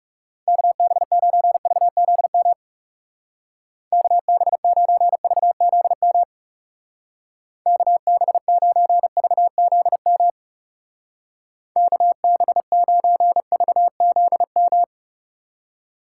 Morse code, heard as a keyed tone, about 38 wpm, 700 Hz. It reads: KB9VZM KB9VZM KB9VZM KB9VZM